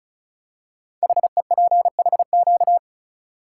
Morse code, heard as a keyed tone, about 36 wpm, 700 Hz. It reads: HEPHQ
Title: Code Groups - 5 Letters - 35wpm